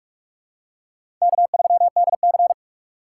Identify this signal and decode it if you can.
Morse code, heard as a keyed tone, about 46 wpm, 690 Hz. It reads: K3DC
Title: USA Call Signs - 3x Character Spacing - 45wpm